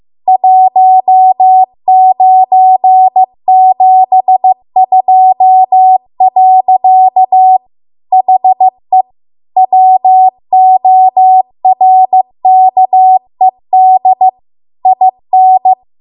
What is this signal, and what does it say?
Morse code, heard as a keyed tone, about 15 wpm, 750 Hz. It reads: 1972. HE WORKED IN
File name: Archive / 15 WPM / 15WPM